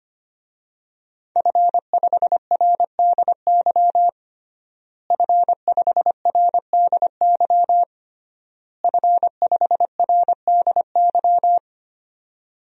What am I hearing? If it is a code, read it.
Morse code, heard as a keyed tone, about 25 wpm, 700 Hz. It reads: F5RDY F5RDY F5RDY
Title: DXCC-100 Call Signs - 5 Characters - Repeated 3x - 25wpm